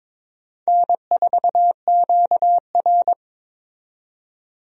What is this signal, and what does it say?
Morse code, heard as a keyed tone, about 22 wpm, 700 Hz. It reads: N4QR
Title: USA Call Signs - 4 Characters - 22wpm